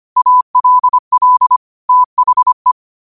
Morse code, heard as a keyed tone, about 25 wpm, 1000 Hz. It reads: ALL THE
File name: qotdfiles / 25wpm